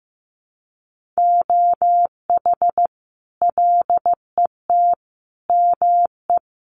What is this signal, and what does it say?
Morse code, heard as a keyed tone, about 15 wpm, 700 Hz. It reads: OH LET ME
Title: Sentences from Top 500 Words 15wpm